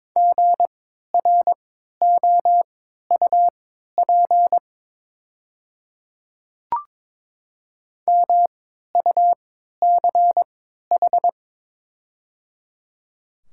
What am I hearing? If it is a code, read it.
Morse code, heard as a keyed tone, about 22 wpm, spaced as for 12 wpm, 690 Hz. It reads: GROUP MUCH
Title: 3x Character Spacing - Top 300 Words - 22wpm